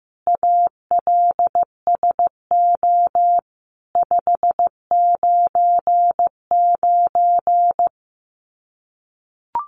Morse code, heard as a keyed tone, about 15 wpm, 700 Hz. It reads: ALSO 599
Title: Common QSO Phrases 15wpm